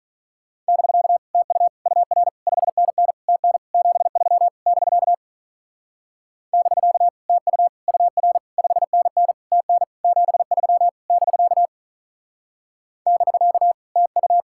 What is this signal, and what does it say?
Morse code, heard as a keyed tone, about 43 wpm, 700 Hz. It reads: <BK> TU UR 5NN TN 73 <BK> <BK> TU UR 5NN TN 73 <BK> <BK> TU
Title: Speed-Racing Common QSO Phrases - 35wpm